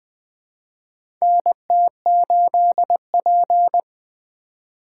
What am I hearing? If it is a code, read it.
Morse code, heard as a keyed tone, about 20 wpm, 700 Hz. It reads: NT8P
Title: USA Call Signs - 4 Characters - 3x Character Spacing - 20wpm